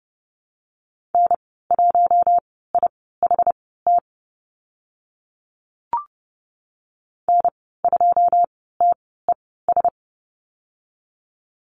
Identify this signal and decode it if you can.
Morse code, heard as a keyed tone, about 30 wpm, spaced as for 17 wpm, 700 Hz. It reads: N1IHT N2TES
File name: Lesson 14g - Number 2 - Call Signs 30wpm